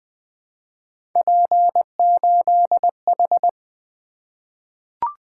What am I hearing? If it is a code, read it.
Morse code, heard as a keyed tone, about 20 wpm, 700 Hz. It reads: P8H
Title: Three Letter-Number Combinations - 20wpm